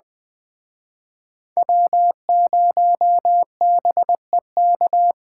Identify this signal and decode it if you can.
Morse code, heard as a keyed tone, about 23 wpm, 700 Hz. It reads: W0BEK
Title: Speed Racing - USA 1x3 Call Signs 20wpm